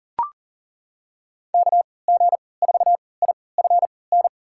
Morse code, heard as a keyed tone, about 40 wpm, spaced as for 23 wpm, 700 Hz. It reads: KG4IFN
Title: USA Call Signs - 3x Character Spacing - 40wpm